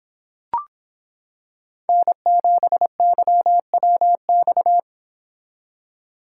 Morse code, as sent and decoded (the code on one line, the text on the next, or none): -. --... -.-- .-- -..-
N7YWX